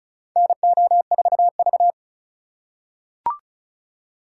Morse code, heard as a keyed tone, about 36 wpm, 700 Hz. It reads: NO4V
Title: USA Call Signs - 4 Characters - 3x Character Spacing - 35wpm